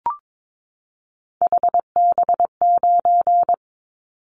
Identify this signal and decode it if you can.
Morse code, heard as a keyed tone, about 22 wpm, 700 Hz. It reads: HB9
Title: Callsign Prefixes for Top 25 DXCC - 22wpm